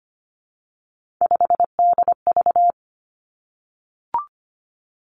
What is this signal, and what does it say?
Morse code, heard as a keyed tone, about 25 wpm, 700 Hz. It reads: 5DV
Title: Three Challenging Letter-Number Combinations 25wpm